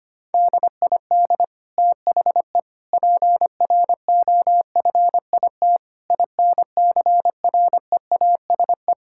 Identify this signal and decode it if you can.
Morse code, heard as a keyed tone, about 25 wpm, 700 Hz. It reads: DID THE PROFIT INCREASE
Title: Sentences from Top 1000 Words - Review Part 5 - 25wpm